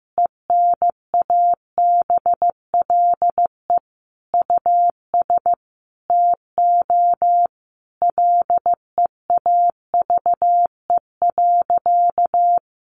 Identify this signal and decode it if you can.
Morse code, heard as a keyed tone, about 15 wpm, 700 Hz. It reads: ENABLE US TO LEAVE.
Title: Sentences from Top 1500 Words 15wpm